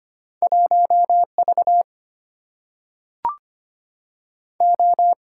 Morse code, heard as a keyed tone, about 25 wpm, 700 Hz. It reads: 1V O
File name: Two Letter-Number Combinations - 25wpm